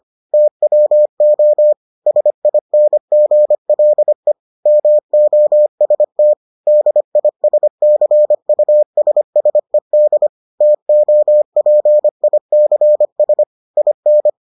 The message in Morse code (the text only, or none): TWO SINGLE MOST DISCUSSED TOPICS IN